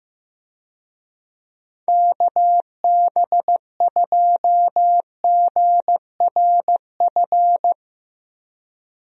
Morse code, heard as a keyed tone, about 15 wpm, 700 Hz. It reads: KB2GRF
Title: USA Call Signs - 6 Characters - 15wpm